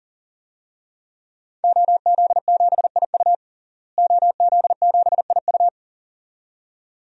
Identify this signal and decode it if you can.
Morse code, heard as a keyed tone, about 40 wpm, 700 Hz. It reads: OZ7IU OZ7IU
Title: DXCC-100 Call Signs - 5 Characters - Repeated 2x - 40wpm